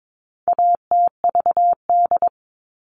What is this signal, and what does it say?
Morse code, heard as a keyed tone, about 22 wpm, 700 Hz. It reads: ATVD
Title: All Cut Numbers - 4-Digit 22wpm